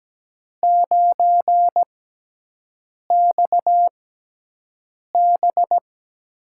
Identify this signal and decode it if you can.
Morse code, heard as a keyed tone, about 17 wpm, spaced as for 6 wpm, 700 Hz. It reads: 9XB
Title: USA Call Signs - 4 Characters - 6x Character Spacing - 17wpm